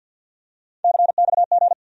Morse code, heard as a keyed tone, about 47 wpm, 700 Hz. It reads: CXG
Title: USA Call Sign Postfixes - 3 Letters 50wpm